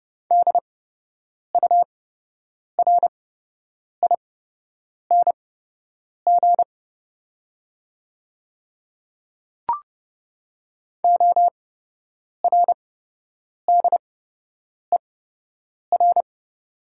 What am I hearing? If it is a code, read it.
Morse code, heard as a keyed tone, about 30 wpm, spaced as for 8 wpm, 700 Hz. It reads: DURING ORDER
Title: 8x Character Spacing - Top 500 Words - 30wpm